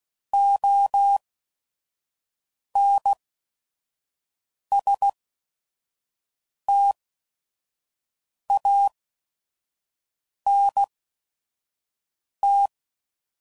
Morse code, heard as a keyed tone, about 16 wpm, spaced as for 5 wpm, 800 Hz. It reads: ONSTANT
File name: CW Code Groups @5WPM / Technical Subjects / XC